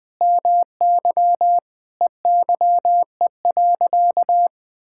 Morse code, heard as a keyed tone, about 20 wpm, 700 Hz. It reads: MY EYE.
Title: Sentences with 7-Word Phrases - 20wpm